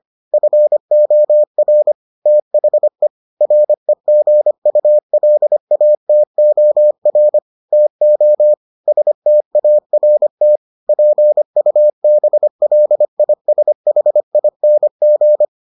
Morse code, heard as a keyed tone, about 25 wpm, 600 Hz. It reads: FOR THE REGULATOR TO START PUBLISHING